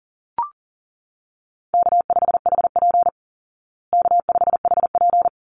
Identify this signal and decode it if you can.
Morse code, heard as a keyed tone, about 40 wpm, 700 Hz. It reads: K5HP K5HP
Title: Call Signs Repeated 2x - 40wpm